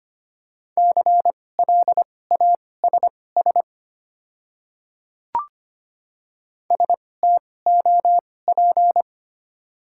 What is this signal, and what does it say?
Morse code, heard as a keyed tone, about 25 wpm, spaced as for 18 wpm, 700 Hz. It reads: CLASS STOP
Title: 2x Character Spacing - Top 500 Words - 25wpm